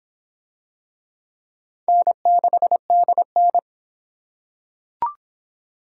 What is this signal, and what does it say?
Morse code, heard as a keyed tone, about 26 wpm, 700 Hz. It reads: N6DN